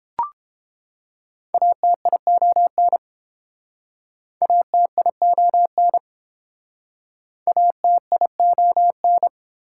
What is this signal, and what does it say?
Morse code, heard as a keyed tone, about 28 wpm, 700 Hz. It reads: ATION ATION ATION